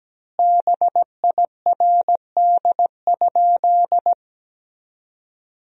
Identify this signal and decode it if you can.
Morse code, heard as a keyed tone, about 17 wpm, 700 Hz. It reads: BIRD?
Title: 3x Word-Spacing - Sentences from Top 400 Words - 17wpm